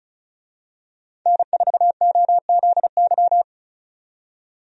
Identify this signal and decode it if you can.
Morse code, heard as a keyed tone, about 36 wpm, 700 Hz. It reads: N4OZY
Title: USA Call Signs - 5 Characters - 35wpm